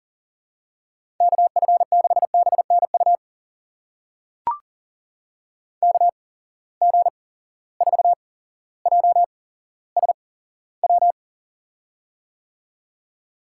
Morse code, heard as a keyed tone, about 42 wpm, spaced as for 17 wpm, 700 Hz. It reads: KF6BNU KG4JSW
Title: USA Call Signs - 6 Characters - 8x Character Spacing - 40wpm